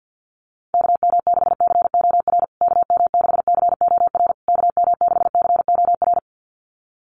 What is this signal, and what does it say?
Morse code, heard as a keyed tone, about 50 wpm, 700 Hz. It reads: KM6COR KM6COR KM6COR
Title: USA Call Signs Repeated 3x 50wpm